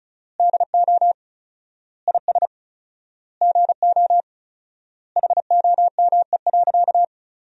Morse code, heard as a keyed tone, about 36 wpm, 700 Hz. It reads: DO IS GO HOME.
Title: 4x Word-Spacing - Sentences from Top 200 Words - 35wpm